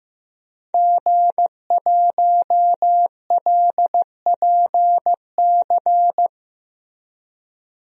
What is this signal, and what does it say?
Morse code, heard as a keyed tone, about 15 wpm, 700 Hz. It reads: G1LPC
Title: DXCC-100 Call Signs - 5 Characters - Repeated 2x - 15wpm